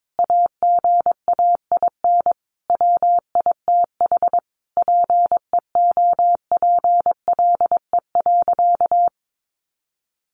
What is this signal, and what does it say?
Morse code, heard as a keyed tone, about 22 wpm, 700 Hz. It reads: AGAIN WITH PEOPLE.
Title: Sentences from Top 1100 Words 22wpm